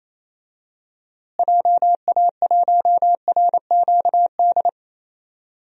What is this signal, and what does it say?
Morse code, heard as a keyed tone, about 28 wpm, 700 Hz. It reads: JA1RQD
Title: DXCC-100 Call Signs - 6 Characters - Repeated 3x - 28wpm